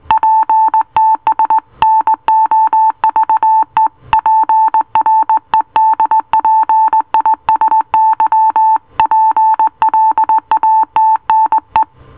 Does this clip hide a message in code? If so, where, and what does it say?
Yes: Morse code from 0.1 to 11.8 s, heard as a keyed tone, about 19 wpm, 880 Hz. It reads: PTS NOVE PREDPISY PLATNE